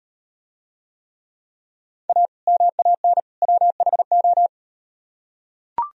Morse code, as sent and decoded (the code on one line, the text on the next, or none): .- / -- .- -. / .-- .... ---
A MAN WHO